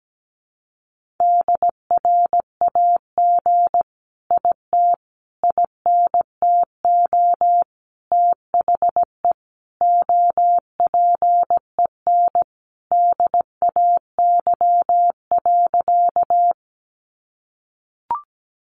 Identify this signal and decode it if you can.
Morse code, heard as a keyed tone, about 17 wpm, 700 Hz. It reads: DRAG IT INTO THE OPEN DAY.
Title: Sentences from Top 1500 Words 17wpm